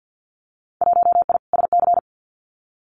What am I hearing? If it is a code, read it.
Morse code, heard as a keyed tone, about 50 wpm, 700 Hz. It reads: 1I SC
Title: Winter Field Day Exchanges Encore 50wpm